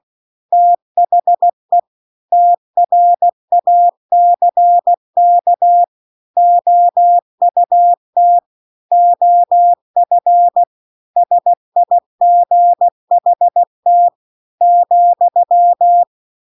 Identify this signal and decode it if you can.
Morse code, heard as a keyed tone, about 16 wpm, 700 Hz. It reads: THE TRACK OUT OF SIGHT ,
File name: CW-The-Big-Twoheaded-River-1-16WPM